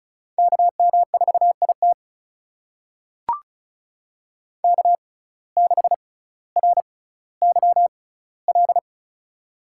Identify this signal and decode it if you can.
Morse code, heard as a keyed tone, about 36 wpm, spaced as for 18 wpm, 700 Hz. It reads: KM4IT K6RYL
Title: USA Call Signs - 5 Characters - 6x Character Spacing - 35wpm